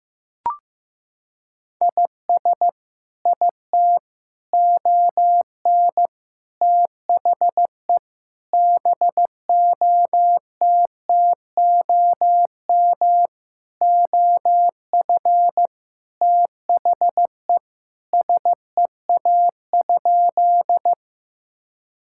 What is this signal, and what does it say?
Morse code, heard as a keyed tone, about 14 wpm, 700 Hz. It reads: IS IT ON THE BOTTOM OF THE SEA?
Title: Sentences from Top 1000 Words - Review Part 6 - 15wpm